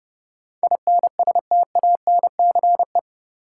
Morse code, heard as a keyed tone, about 31 wpm, 700 Hz. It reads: INSTANCE
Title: Sentences from Top 800 Words - New Word Drill - 30wpm